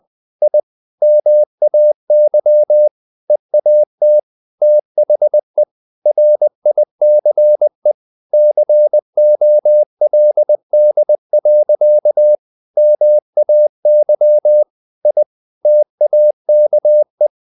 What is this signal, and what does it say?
Morse code, heard as a keyed tone, about 20 wpm, 600 Hz. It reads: I MAY EAT THE RICE COLD. MAY I TAKE
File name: the old man and the sea / the old man and the sea part 1 20wpm